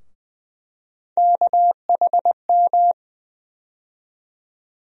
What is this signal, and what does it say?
Morse code, heard as a keyed tone, about 20 wpm, 700 Hz. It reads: KHM
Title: USA Call Sign Postfixes - 3 Letters 20wpm